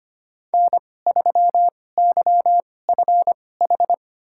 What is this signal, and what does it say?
Morse code, heard as a keyed tone, about 25 wpm, spaced as for 18 wpm, 700 Hz. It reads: N3YFH
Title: USA Call Signs - 5 Characters - 2x Character Spacing - 25wpm